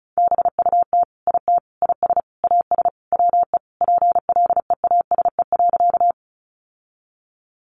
Morse code, heard as a keyed tone, about 34 wpm, 700 Hz. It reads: BUT IT IS AS WE PLEASE.
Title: Sentences from Top 1500 Words 35wpm